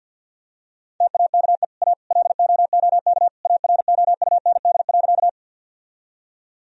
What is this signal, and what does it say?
Morse code, heard as a keyed tone, about 50 wpm, 700 Hz. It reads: TAKE A LOOK AROUND.
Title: Sentences from Top 1000 Words - Review Part 3 - 50wpm